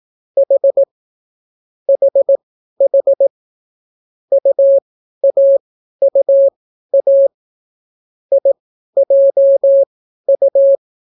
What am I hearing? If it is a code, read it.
Morse code, heard as a keyed tone, about 18 wpm, spaced as for 12 wpm, 550 Hz. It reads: H HH UAUA IJU